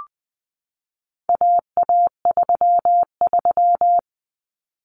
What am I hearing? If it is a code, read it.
Morse code, heard as a keyed tone, about 20 wpm, 700 Hz. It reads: AA33
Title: Common Cut Numbers - 4-Digit 20wpm